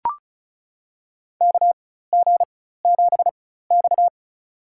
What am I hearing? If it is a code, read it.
Morse code, heard as a keyed tone, about 36 wpm, spaced as for 16 wpm, 700 Hz. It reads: KG7X
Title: USA Call Signs - 4 Characters - 4x Character Spacing - 35wpm